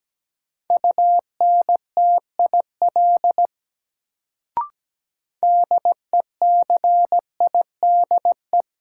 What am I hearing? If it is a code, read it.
Morse code, heard as a keyed tone, about 17 wpm, 700 Hz. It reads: UNTIL DECIDE
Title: Sentences from Top 500 Words - New Word Drill - 17wpm